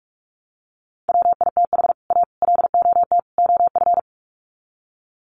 Morse code, heard as a keyed tone, about 44 wpm, 700 Hz. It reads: WITH A LOT OF
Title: Sets of 4 Words 45wpm